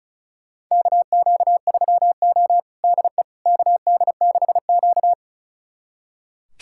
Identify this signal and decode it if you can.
Morse code, heard as a keyed tone, about 36 wpm, 700 Hz. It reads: KQ3O DE KD6Q
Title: USA Callsign DE Callsign - 4 Characters - 35wpm